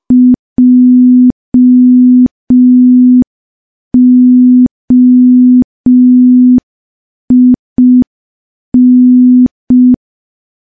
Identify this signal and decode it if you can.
Morse code, heard as a keyed tone, about 5 wpm, 250 Hz. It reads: JOIN
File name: morse / morse-training-05WPM